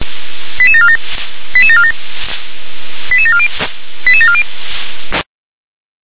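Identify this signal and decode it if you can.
Five-tone selective call (ZVEI), heard as quick runs of five tones, tones of about 70 ms. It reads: 88747 88747 88740 88740
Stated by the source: Received on SDR, ZVEI five-tone call.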